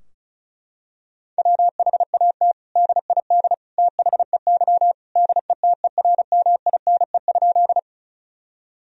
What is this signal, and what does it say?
Morse code, heard as a keyed tone, about 34 wpm, 700 Hz. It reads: WHAT DID THEY DETERMINE?
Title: Sentences from Top 1000 Words - Review Part 4 - 35wpm